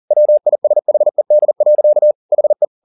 Morse code, heard as a keyed tone, about 40 wpm, 600 Hz. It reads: WISHED. HE
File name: the old man and the sea 2 40wpm